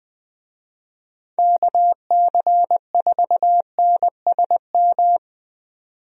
Morse code, heard as a keyed tone, about 20 wpm, 700 Hz. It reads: KC4NSM
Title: USA Call Signs - 6 Characters - 20wpm